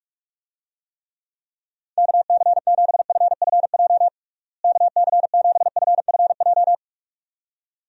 Morse code, heard as a keyed tone, about 47 wpm, 700 Hz. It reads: KC7FFJ KC7FFJ
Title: USA Call Signs - 6 Characters - Repeated 2x - 45wpm